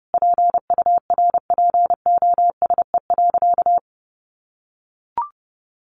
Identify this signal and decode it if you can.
Morse code, heard as a keyed tone, about 30 wpm, 700 Hz. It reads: PURPOSE.
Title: Sentences from Top 800 Words 30wpm